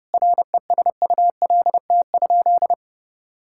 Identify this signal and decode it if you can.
Morse code, heard as a keyed tone, about 32 wpm, 700 Hz. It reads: RESULT?
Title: Sentences from Top 1000 Words - Review Part 5 - 30wpm